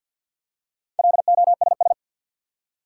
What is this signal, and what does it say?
Morse code, heard as a keyed tone, about 47 wpm, 700 Hz. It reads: LOSS
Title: Four Letter Words - 50wpm